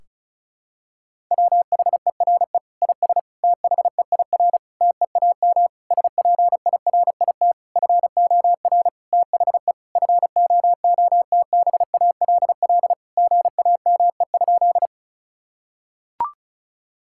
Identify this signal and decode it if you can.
Morse code, heard as a keyed tone, about 36 wpm, 700 Hz. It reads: WHERE IS THEIR TEAM SPIRIT FOR THE FOOTBALL GAME?
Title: Sentences from Top 1600 Words 35wpm